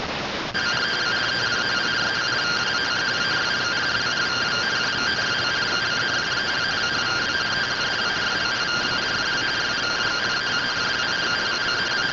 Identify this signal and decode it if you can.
Two-tone teleprinter signal, about 170 Hz shift.